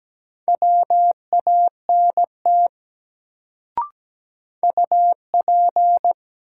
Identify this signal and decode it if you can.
Morse code, heard as a keyed tone, about 17 wpm, 700 Hz. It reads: WANT UP
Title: Sets of 2 Words - New Word Drill - 17wpm